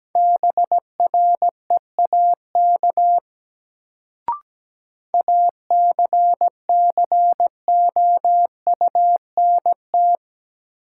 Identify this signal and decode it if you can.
Morse code, heard as a keyed tone, about 17 wpm, 700 Hz. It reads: BREAK ACCOUNT
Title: Sentences from Top 900 Words - New Word Drill - 17wpm